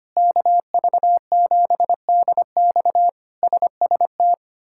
Morse code, heard as a keyed tone, about 25 wpm, 700 Hz. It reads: KV7DX SST
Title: K1USN SST QSO Practice - 25wpm